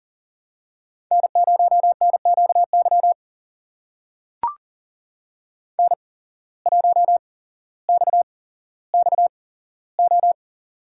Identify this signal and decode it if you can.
Morse code, heard as a keyed tone, about 42 wpm, spaced as for 17 wpm, 700 Hz. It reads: N0NQY N1XXO
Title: USA Call Signs - 5 Characters - 8x Character Spacing - 40wpm